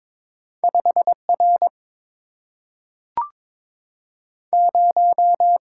Morse code, heard as a keyed tone, about 23 wpm, 690 Hz. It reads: 5R 0
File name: practice / 22wpm / Two Letter-Number Combinations - 22wpm